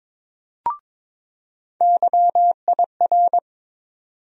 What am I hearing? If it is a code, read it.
Morse code, heard as a keyed tone, about 22 wpm, 700 Hz. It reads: YIR
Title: USA Call Sign Postfixes - 3 Letters 22wpm